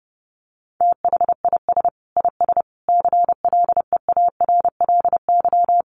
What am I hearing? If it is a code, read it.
Morse code, heard as a keyed tone, about 30 wpm, 700 Hz. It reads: THIS IS CLEARLY
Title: Sentences from Top 1300 Words 30wpm